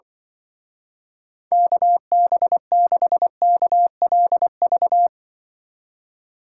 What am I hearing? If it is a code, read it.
Morse code, heard as a keyed tone, about 24 wpm, 700 Hz. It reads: KB6KLV
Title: Speed Racing - USA Call Signs 20wpm